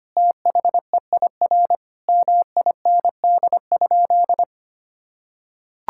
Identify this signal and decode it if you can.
Morse code, heard as a keyed tone, about 25 wpm, 700 Hz. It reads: THEIR MIND?
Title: Sentences from Top 1000 Words - Review Part 5 - 25wpm